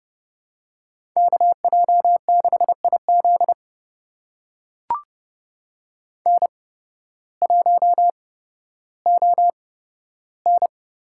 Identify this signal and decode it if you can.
Morse code, heard as a keyed tone, about 30 wpm, spaced as for 14 wpm, 700 Hz. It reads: KJ6IZ N1ON
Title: USA Call Signs - 5 Characters - 8x Character Spacing - 30wpm